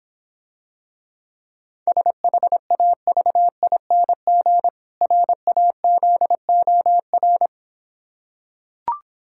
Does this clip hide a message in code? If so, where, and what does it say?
Yes: Morse code from 1.9 to 7.5 s, heard as a keyed tone, about 26 wpm, 700 Hz. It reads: SHAVING RAZOR